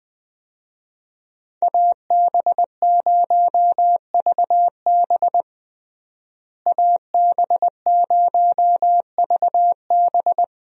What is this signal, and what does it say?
Morse code, heard as a keyed tone, about 21 wpm, 700 Hz. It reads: AB0VB AB0VB
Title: USA Call Signs - 5 Characters - Repeated 2x - 20wpm